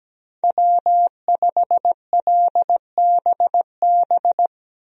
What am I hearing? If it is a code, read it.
Morse code, heard as a keyed tone, about 17 wpm, 700 Hz. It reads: W5LBB
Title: USA Call Signs - 5 Characters - 17wpm